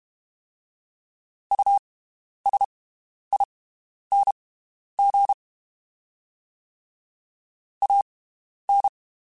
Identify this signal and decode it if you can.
Morse code, heard as a keyed tone, about 32 wpm, spaced as for 11 wpm, 800 Hz. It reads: USING AN
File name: (spaced) / QSOs Spaced-mp3s / WD0FBT 1 @30 sp